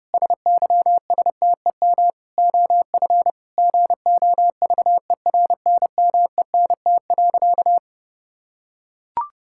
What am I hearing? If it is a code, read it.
Morse code, heard as a keyed tone, about 30 wpm, 700 Hz. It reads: SYSTEM OF GOVERNMENT.
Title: Sentences from Top 1000 Words - Review Part 5 - 30wpm